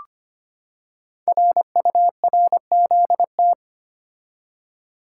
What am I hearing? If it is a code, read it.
Morse code, heard as a keyed tone, about 25 wpm, 700 Hz. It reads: RURZT
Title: Code Groups - 5 Letters - 25wpm